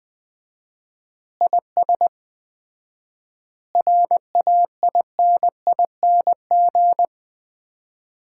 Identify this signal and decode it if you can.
Morse code, heard as a keyed tone, about 21 wpm, 700 Hz. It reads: IS RAINING
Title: Common QSO Phrases - 4x Word-Spacing - 20wpm